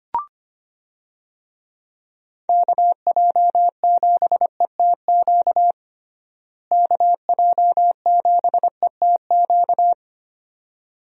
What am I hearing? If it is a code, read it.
Morse code, heard as a keyed tone, about 25 wpm, 700 Hz. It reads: KJ7ETQ KJ7ETQ
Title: USA Call Signs - 6 Characters - Repeated 2x - 25wpm